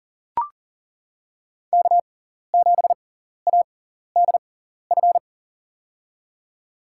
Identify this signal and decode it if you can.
Morse code, heard as a keyed tone, about 40 wpm, spaced as for 14 wpm, 700 Hz. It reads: K7ADF
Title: USA Call Signs - 6x Character Spacing - 40wpm